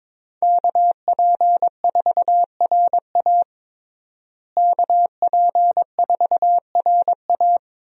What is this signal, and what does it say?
Morse code, heard as a keyed tone, about 22 wpm, 700 Hz. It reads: KP4RA KP4RA
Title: DXCC-100 Call Signs - 5 Characters - Repeated 3x - 22wpm